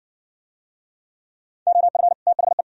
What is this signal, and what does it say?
Morse code, heard as a keyed tone, about 52 wpm, 700 Hz. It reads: OF THE